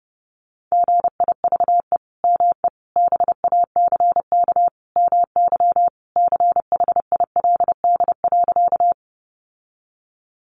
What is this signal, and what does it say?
Morse code, heard as a keyed tone, about 30 wpm, 700 Hz. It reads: GIVE ME BACK MY CHILD.
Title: Sentences from Top 800 Words 30wpm